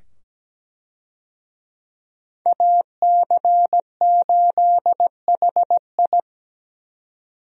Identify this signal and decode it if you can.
Morse code, heard as a keyed tone, about 17 wpm, 700 Hz. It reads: AC8HI
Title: USA Call Signs - 5 Characters - 17wpm